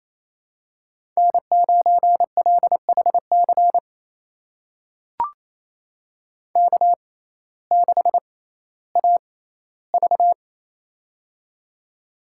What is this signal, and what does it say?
Morse code, heard as a keyed tone, about 28 wpm, spaced as for 15 wpm, 700 Hz. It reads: N9LHC K6AV
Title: USA Call Signs - 6x Character Spacing - 28wpm